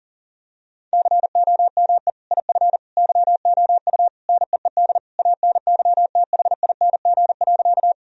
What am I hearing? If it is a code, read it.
Morse code, heard as a keyed tone, about 37 wpm, 700 Hz. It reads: COME IF YOU NEED ANYTHING.
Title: Sentences from Top 1000 Words - Review Part 1 - 40wpm